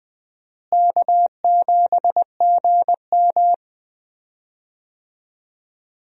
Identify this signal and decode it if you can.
Morse code, heard as a keyed tone, about 20 wpm, 700 Hz. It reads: K7GM
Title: USA Call Signs - 4 Characters - Repeated 2x - 20wpm